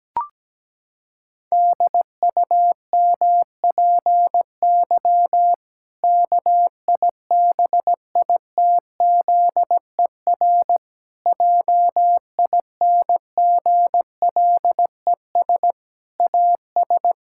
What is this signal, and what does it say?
Morse code, heard as a keyed tone, about 17 wpm, 700 Hz. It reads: DUMPY KIBITZER JINGLES AS
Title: Pangram Challenge v2 - 17wpm